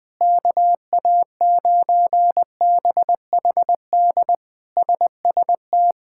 Morse code, heard as a keyed tone, about 20 wpm, 700 Hz. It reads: KA9BHD SST
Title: K1USN SST QSO Practice - 20wpm